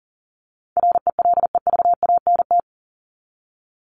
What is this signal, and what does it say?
Morse code, heard as a keyed tone, about 37 wpm, 700 Hz. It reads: RELEVANT
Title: Top 1001-1250 Words 40wpm